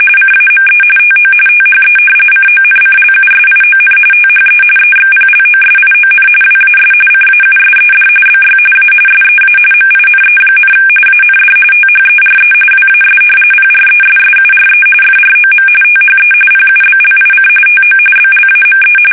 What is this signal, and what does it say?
RTTY, about 45.45 baud. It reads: CAME TO THE CONCLUSION THE EXPLOSION WAS CAUSED BY A SEA MINE, AND THERE WAS PRESSURE FROM THE PUBLIC TO BLAME
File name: fsk850 / usnavy3